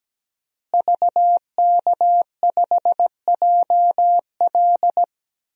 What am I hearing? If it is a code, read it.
Morse code, heard as a keyed tone, about 17 wpm, 700 Hz. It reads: VK5JL
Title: DXCC-100 Call Signs - 5 Characters - Repeated 3x - 17wpm